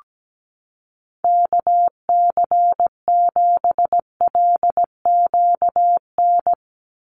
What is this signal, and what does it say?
Morse code, heard as a keyed tone, about 17 wpm, 700 Hz. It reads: KC7LQN
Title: USA Call Signs 17wpm